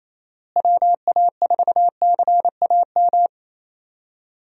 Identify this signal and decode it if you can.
Morse code, heard as a keyed tone, about 28 wpm, 700 Hz. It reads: WA4CAM
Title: USA Call Signs - 6 Characters - Repeated 2x - 28wpm